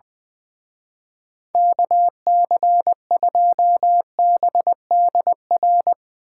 Morse code, heard as a keyed tone, about 20 wpm, 700 Hz. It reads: KC2BDR
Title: Speed Racing - USA 2x3 Call Signs 20wpm